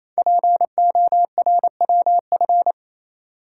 Morse code, heard as a keyed tone, about 27 wpm, 700 Hz. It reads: PORWF
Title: Code Groups - 5 Letters - 28wpm